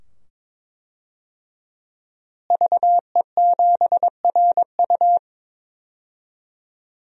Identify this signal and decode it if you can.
Morse code, heard as a keyed tone, about 22 wpm, 700 Hz. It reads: VE7RU
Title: DXCC-100 Call Signs - 5 Characters - Repeated 2x - 22wpm